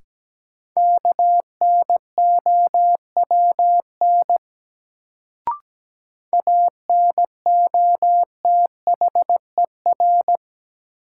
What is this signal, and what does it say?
Morse code, heard as a keyed tone, about 17 wpm, 700 Hz. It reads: KNOWN ANOTHER
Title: Sentences from Top 600 Words - New Word Drill - 17wpm